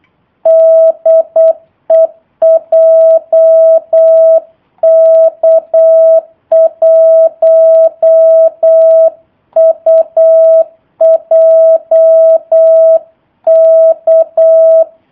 Morse code, heard as a keyed tone, about 8 wpm, 650 Hz. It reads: DE JK1UJK
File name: jh8cbh / 2019cw / sp03- w